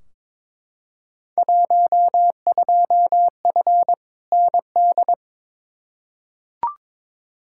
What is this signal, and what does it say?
Morse code, heard as a keyed tone, about 22 wpm, 700 Hz. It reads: 12F ND
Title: Field Day Exchanges Encore - 22wpm